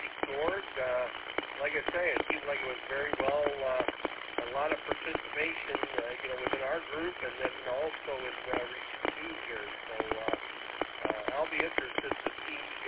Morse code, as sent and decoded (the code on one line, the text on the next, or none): none